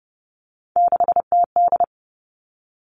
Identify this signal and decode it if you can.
Morse code, heard as a keyed tone, about 30 wpm, 700 Hz. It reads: 6TD
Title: Three Challenging Letter-Number Combinations 30wpm